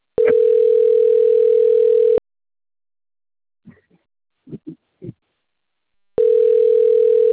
Ringback tone (North America), 440 + 480 Hz.
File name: rg-1600-unknown- - - .41016